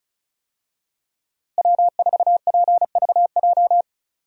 Morse code, heard as a keyed tone, about 35 wpm, 700 Hz. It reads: W4PVJ
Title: USA Call Signs - 5 Characters - Repeated 2x - 35wpm